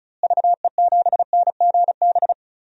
Morse code, heard as a keyed tone, about 36 wpm, 700 Hz. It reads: VE7NGB
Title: DXCC-100 Call Signs - 6 Characters - Repeated 2x - 35wpm